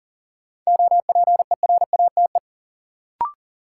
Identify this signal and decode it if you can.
Morse code, heard as a keyed tone, about 40 wpm, 700 Hz. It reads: OPERATE
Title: Sentences from Top 1500 Words - New Word Drill - 40wpm